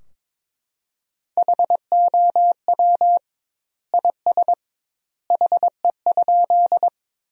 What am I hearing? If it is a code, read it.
Morse code, heard as a keyed tone, about 22 wpm, 700 Hz. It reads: HOW IS HE?
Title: 2x Word-Spacing - Sentences from Top 100 Words - 22wpm